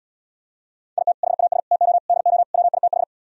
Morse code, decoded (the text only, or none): IT HAS TO DO WITH